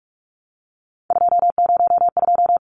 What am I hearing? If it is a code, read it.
Morse code, heard as a keyed tone, about 44 wpm, 700 Hz. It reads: 202
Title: Numbers - 3-Digit 45wpm